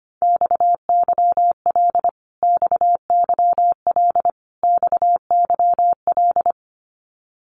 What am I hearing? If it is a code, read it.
Morse code, heard as a keyed tone, about 25 wpm, 700 Hz. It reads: XYL XYL XYL
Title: Abbreviations Repeated 3x - 25wpm